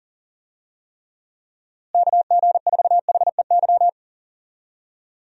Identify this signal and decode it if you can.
Morse code, heard as a keyed tone, about 37 wpm, 700 Hz. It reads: KG4HEY
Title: USA Call Signs - 6 Characters - Repeated 3x - 40wpm